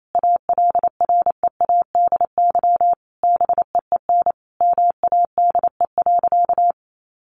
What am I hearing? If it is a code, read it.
Morse code, heard as a keyed tone, about 28 wpm, 700 Hz. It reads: ALREADY BEEN MADE.
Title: Sentences from Top 1400 Words 28wpm